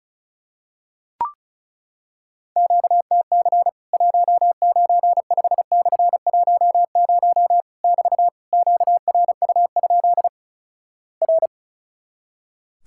Morse code, heard as a keyed tone, about 36 wpm, 700 Hz. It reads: QTC 195/10 <BT> QRU? R
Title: Worked All Europe DX Contest - QTC Exchange - 35wpm